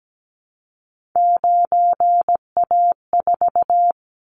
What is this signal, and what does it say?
Morse code, heard as a keyed tone, about 17 wpm, 700 Hz. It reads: 9A4
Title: Callsign Prefixes for Top 100 DXCC - 17wpm